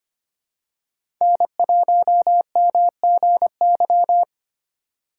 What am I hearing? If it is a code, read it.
Morse code, heard as a keyed tone, about 25 wpm, 700 Hz. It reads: N1MGY
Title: USA Call Signs - 4x Character Spacing - 25wpm